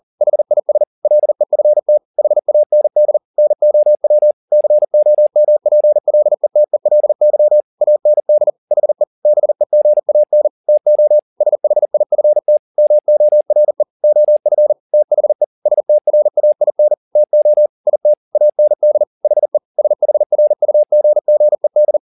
Morse code, heard as a keyed tone, about 40 wpm, 600 Hz. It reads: HIS LEFT HAND NOW COMPLETELY AND HE BEGAN TO SHIFT MORE OF THE STRAIN TO IT AND HE SHRUGGED